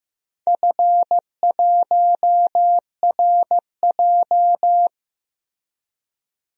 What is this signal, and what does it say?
Morse code, heard as a keyed tone, about 15 wpm, 700 Hz. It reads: F1RJ
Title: DXCC-100 Call Signs - 4 Characters - 15wpm